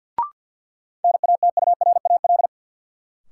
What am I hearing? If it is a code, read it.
Morse code, heard as a keyed tone, about 52 wpm, 700 Hz. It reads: NATURAL
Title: Sentences from Top 700 Words - New Word Drill - 50wpm